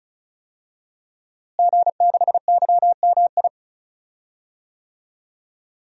Morse code, heard as a keyed tone, about 36 wpm, 700 Hz. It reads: G6YMI
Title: DXCC-100 Call Signs - 5 Characters - Repeated 2x - 35wpm